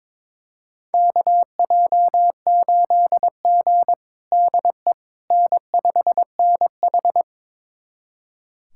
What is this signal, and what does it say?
Morse code, heard as a keyed tone, about 22 wpm, 700 Hz. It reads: KJ8G DE N5NH
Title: USA Callsign DE Callsign - 4 Characters - 22wpm